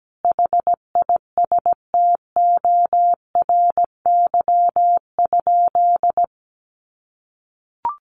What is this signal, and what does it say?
Morse code, heard as a keyed tone, about 17 wpm, 700 Hz. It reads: HISTORY?
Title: Sentences from Top 600 Words 17wpm